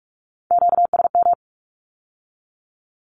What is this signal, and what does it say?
Morse code, heard as a keyed tone, about 46 wpm, 700 Hz. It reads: QSM
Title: Q-Codes with Spoken Meaning - 45wpm